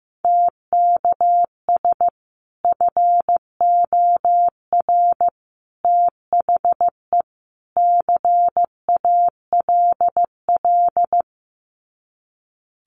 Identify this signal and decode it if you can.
Morse code, heard as a keyed tone, about 14 wpm, 700 Hz. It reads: TKS FOR THE CALL
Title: Common QSO Phrases 15wpm